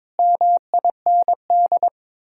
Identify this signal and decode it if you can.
Morse code, heard as a keyed tone, about 22 wpm, 700 Hz. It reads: MIND
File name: practice / 22wpm / Sets of 7 Words - 22wpm